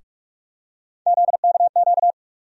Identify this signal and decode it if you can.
Morse code, heard as a keyed tone, about 46 wpm, 700 Hz. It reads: ZKQ
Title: USA Call Sign Postfixes - 3 Letters 45wpm